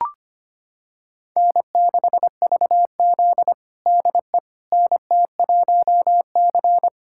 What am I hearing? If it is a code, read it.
Morse code, heard as a keyed tone, about 25 wpm, 700 Hz. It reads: N6VZ DE NT1C
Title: USA Callsign DE Callsign - 4 Characters - 25wpm